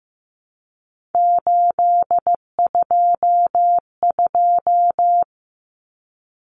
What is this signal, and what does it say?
Morse code, heard as a keyed tone, about 15 wpm, 700 Hz. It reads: 822
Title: Common Cut Numbers - 3-Digit 15wpm